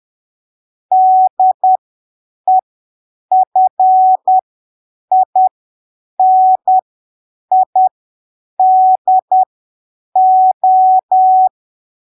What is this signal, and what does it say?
Morse code, heard as a keyed tone, about 10 wpm, spaced as for 7 wpm, 750 Hz. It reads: DEFINIDO
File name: arquivos / texto / texto 01 10ppm